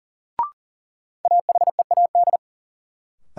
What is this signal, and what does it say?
Morse code, heard as a keyed tone, about 39 wpm, 700 Hz. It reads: AHEAD
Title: Sentences from Top 1000 Words - Review Part 6 - New Word Drill - 40wpm